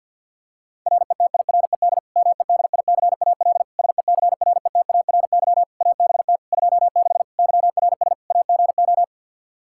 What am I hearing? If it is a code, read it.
Morse code, heard as a keyed tone, about 46 wpm, 700 Hz. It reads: RETIRED MEDICAL SECRETARY ABT 26 YRS AGO
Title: K1USN SST QSO Practice 50wpm